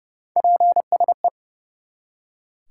Morse code, heard as a keyed tone, about 30 wpm, 700 Hz. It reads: PSE
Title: Top 90 QSO Elements 30wpm